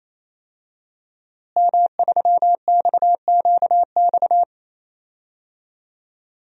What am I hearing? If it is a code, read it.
Morse code, heard as a keyed tone, about 28 wpm, 700 Hz. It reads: M3XQX
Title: DXCC-100 Call Signs - 5 Characters - Repeated 3x - 28wpm